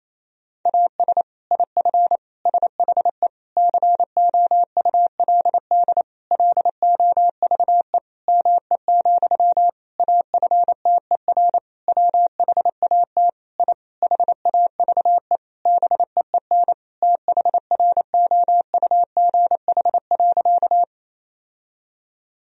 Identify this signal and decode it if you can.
Morse code, heard as a keyed tone, about 28 wpm, 700 Hz. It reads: AS IF SHE COULD LOVE ME, AFTER WHAT I HAVE BEEN THROUGH.
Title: Sentences from Top 1000 Words - Review Part 1 - 28wpm